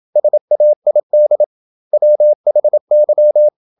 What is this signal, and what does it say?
Morse code, heard as a keyed tone, about 27 wpm, 600 Hz. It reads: SAID WHY
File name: Never 0027wpm